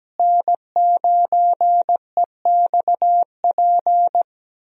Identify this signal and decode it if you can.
Morse code, heard as a keyed tone, about 17 wpm, 700 Hz. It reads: N9EXP
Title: USA Call Signs - 5 Characters - Repeated 3x - 17wpm